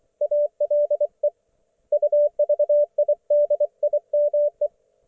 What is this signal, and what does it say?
Morse code, heard as a keyed tone, about 24 wpm, 590 Hz. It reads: ALE UVIDIME
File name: Archiv / CUC212 / ok1fhi cul212